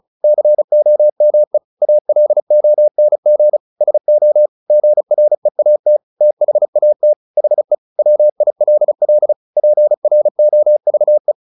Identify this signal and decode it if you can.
Morse code, heard as a keyed tone, about 34 wpm, 600 Hz. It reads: COME ALONG SO GREAT THAT HE WILL PROVE
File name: the old man and the sea / the old man and the sea 1 35wpm